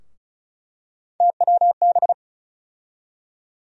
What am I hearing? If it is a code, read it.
Morse code, heard as a keyed tone, about 36 wpm, 700 Hz. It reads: TWB
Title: USA Call Sign Postfixes - 3 Letters 35wpm